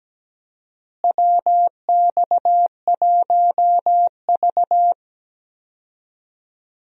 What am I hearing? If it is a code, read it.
Morse code, heard as a keyed tone, about 17 wpm, 700 Hz. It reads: WX1V
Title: USA Call Signs - 4 Characters - Repeated 3x - 17wpm